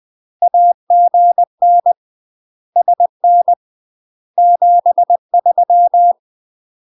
Morse code, heard as a keyed tone, about 20 wpm, 700 Hz. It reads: AGN SN 73
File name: UK MASTER QSOs TRAINING / 20WPM / UK MASTER QSOs 0060